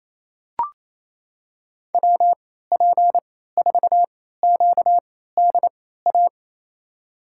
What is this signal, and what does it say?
Morse code, heard as a keyed tone, about 28 wpm, spaced as for 16 wpm, 700 Hz. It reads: WP4QDA
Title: USA Call Signs - 3x Character Spacing - 28wpm